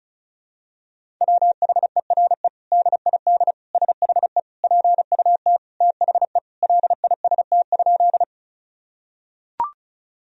Morse code, heard as a keyed tone, about 36 wpm, 700 Hz. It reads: WHERE DID SHE PUT THE LIST?
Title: Sentences from Top 1000 Words - Review Part 2 - 35wpm